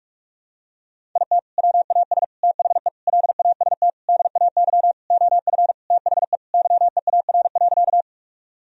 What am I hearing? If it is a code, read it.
Morse code, heard as a keyed tone, about 47 wpm, 700 Hz. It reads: IT WAS THE LAST DAY OF THE YEAR.
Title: Sentences from Top 1000 Words - Review Part 1 - 45wpm